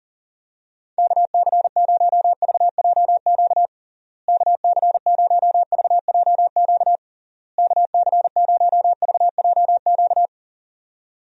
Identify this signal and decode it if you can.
Morse code, heard as a keyed tone, about 40 wpm, 700 Hz. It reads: KC0VJQ KC0VJQ KC0VJQ
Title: USA Call Signs - 6 Characters - Repeated 3x - 40wpm